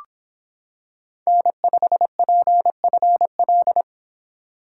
Morse code, heard as a keyed tone, about 26 wpm, 700 Hz. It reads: N5PFL